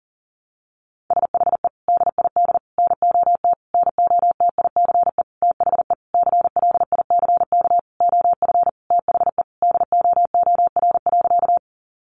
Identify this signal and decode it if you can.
Morse code, heard as a keyed tone, about 40 wpm, 700 Hz. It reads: SHE DID NOT NOTICE THE CLICK OF THE DOOR.
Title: Sentences from Top 1100 Words 40wpm